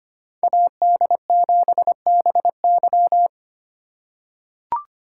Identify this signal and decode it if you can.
Morse code, heard as a keyed tone, about 25 wpm, 700 Hz. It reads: AD7BY
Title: USA Call Signs - 2x Character Spacing - 25wpm